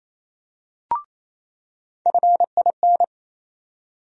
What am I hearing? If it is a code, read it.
Morse code, heard as a keyed tone, about 28 wpm, 700 Hz. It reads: FIN
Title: USA Call Sign Postfixes - 3 Letters 28wpm